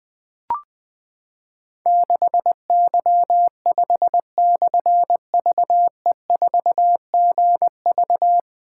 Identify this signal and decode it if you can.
Morse code, heard as a keyed tone, about 20 wpm, 700 Hz. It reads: 6Y5/VE4GV
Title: Difficult Call Signs 20wpm